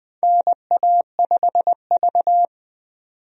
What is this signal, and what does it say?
Morse code, heard as a keyed tone, about 20 wpm, 700 Hz. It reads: NA5V
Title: USA Call Signs - 4 Characters - 20wpm